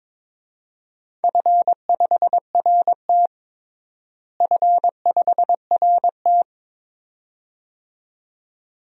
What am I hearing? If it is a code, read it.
Morse code, heard as a keyed tone, about 21 wpm, 700 Hz. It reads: F5RT F5RT
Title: DXCC-100 Call Signs - 4 Characters - Repeated 2x - 22wpm